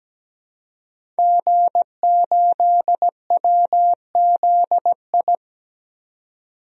Morse code, heard as a keyed tone, about 17 wpm, 700 Hz. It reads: G8WZI
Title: DXCC-100 Call Signs - 5 Characters - 17wpm